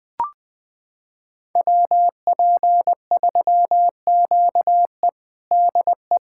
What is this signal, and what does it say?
Morse code, heard as a keyed tone, about 20 wpm, 700 Hz. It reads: WP3QE DE
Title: USA Callsign DE Callsign - 5 Characters - 20wpm